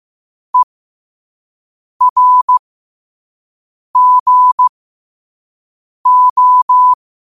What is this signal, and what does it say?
Morse code, heard as a keyed tone, about 15 wpm, spaced as for 5 wpm, 1000 Hz. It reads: ERGO